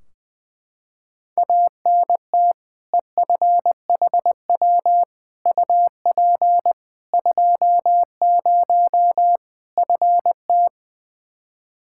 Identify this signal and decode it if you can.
Morse code, heard as a keyed tone, about 20 wpm, 700 Hz. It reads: ANT EFHW UP 20 FT
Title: Practice QSO - Exchange 2 - Info - 20wpm